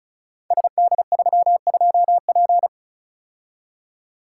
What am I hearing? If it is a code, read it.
Morse code, heard as a keyed tone, about 35 wpm, 690 Hz. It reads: SD32P
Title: Code Groups - 5 Letter:Numbers - 35wpm